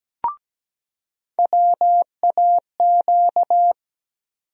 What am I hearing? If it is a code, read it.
Morse code, heard as a keyed tone, about 17 wpm, 700 Hz. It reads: WAQ
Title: USA Call Sign Postfixes - 3 Letters 17wpm